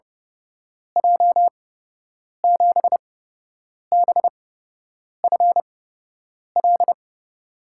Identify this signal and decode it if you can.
Morse code, heard as a keyed tone, about 30 wpm, spaced as for 8 wpm, 700 Hz. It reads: J7BFL
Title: USA Call Signs - 8x Character Spacing - 30wpm